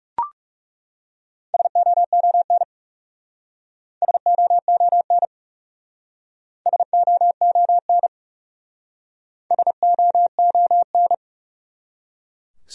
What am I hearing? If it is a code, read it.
Morse code, heard as a keyed tone, about 38 wpm, 700 Hz. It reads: SOON SOON SOON SOON